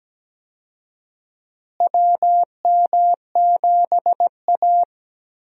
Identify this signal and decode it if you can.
Morse code, heard as a keyed tone, about 17 wpm, 700 Hz. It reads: WM7A
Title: USA Call Signs - 4 Characters - Repeated 3x - 17wpm